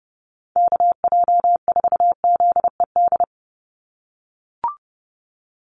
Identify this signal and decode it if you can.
Morse code, heard as a keyed tone, about 30 wpm, 700 Hz. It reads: KJ4ZED
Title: USA Call Signs 30wpm